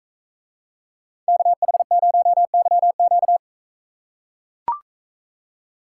Morse code, heard as a keyed tone, about 43 wpm, 700 Hz. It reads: KH0YQ